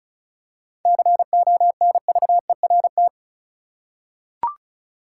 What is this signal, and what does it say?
Morse code, heard as a keyed tone, about 36 wpm, 700 Hz. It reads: CONVERT
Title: Seven Letter Words - 35wpm